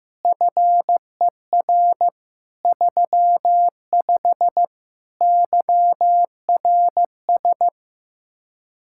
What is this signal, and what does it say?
Morse code, heard as a keyed tone, about 15 wpm, 700 Hz. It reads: FER 35 YRS
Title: Speed-Racing Common QSO Phrases Encore - 15wpm